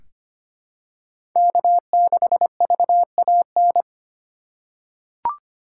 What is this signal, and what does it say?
Morse code, heard as a keyed tone, about 25 wpm, 700 Hz. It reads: K6VAN
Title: USA Call Signs - 3x Character Spacing - 25wpm